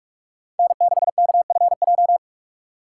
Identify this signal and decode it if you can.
Morse code, heard as a keyed tone, about 46 wpm, 700 Hz. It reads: N6KFJ
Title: USA Call Signs - 3x Character Spacing - 45wpm